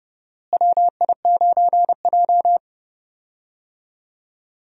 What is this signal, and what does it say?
Morse code, heard as a keyed tone, about 31 wpm, 700 Hz. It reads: WI9J
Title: USA Call Signs - 4 Characters - Repeated 3x - 30wpm